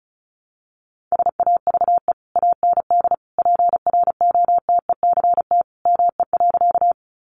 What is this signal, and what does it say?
Morse code, heard as a keyed tone, about 34 wpm, 700 Hz. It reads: SAVE AND PROTECT ME.
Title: Sentences from Top 1100 Words 35wpm